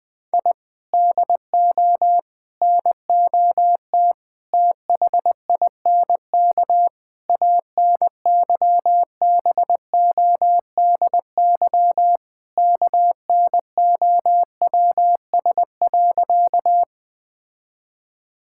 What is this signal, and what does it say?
Morse code, heard as a keyed tone, about 21 wpm, 700 Hz. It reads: I DO NOT THINK ANYBODY KNOWS.
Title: Sentences with 5-Word Phrases - Spoken Beginning - 20wpm